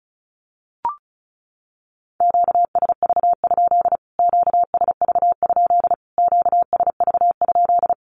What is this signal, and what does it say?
Morse code, heard as a keyed tone, about 34 wpm, 700 Hz. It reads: QSV? QSV? QSV?
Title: Q-Codes Repeated 3x 35wpm